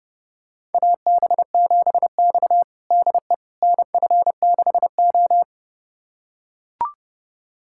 Morse code, heard as a keyed tone, about 30 wpm, 700 Hz. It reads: AB7X DE NF6O
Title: USA Callsign DE Callsign - 4 Characters - 30wpm